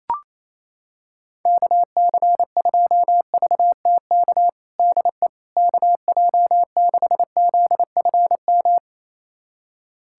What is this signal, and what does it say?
Morse code, heard as a keyed tone, about 28 wpm, 700 Hz. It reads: KC2VTK DE KJ6ZFM
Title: USA Callsign DE Callsign 28wpm